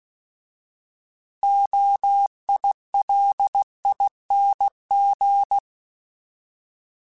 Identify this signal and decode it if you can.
Morse code, heard as a keyed tone, about 16 wpm, 800 Hz. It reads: OILING